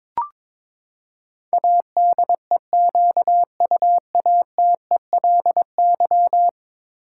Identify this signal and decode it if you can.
Morse code, heard as a keyed tone, about 22 wpm, 700 Hz. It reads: ADEQUATELY
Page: morse-code-ninja-practice-22wpm